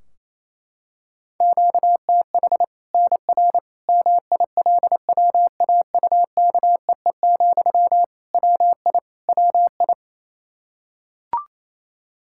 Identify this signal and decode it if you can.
Morse code, heard as a keyed tone, about 28 wpm, 700 Hz. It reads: QTH NR MILWAUKEE, WI WI
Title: K1USN SST QSO Practice 28wpm